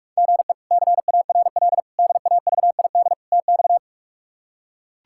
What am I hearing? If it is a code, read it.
Morse code, heard as a keyed tone, about 44 wpm, 700 Hz. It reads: GE CARL DAVID TX
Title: K1USN SST QSO Practice - 45wpm